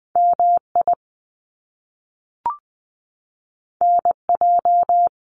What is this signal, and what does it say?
Morse code, heard as a keyed tone, about 20 wpm, 700 Hz. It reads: MI NJ
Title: US States Repeated 3x 20wpm